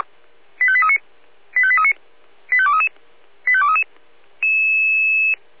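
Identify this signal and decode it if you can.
Five-tone selective call (ZVEI), heard as quick runs of five tones, tones of about 70 ms. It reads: 85839 85839 85233 85233
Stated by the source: Five-tone ZVEI selective call 85230 received by SDR.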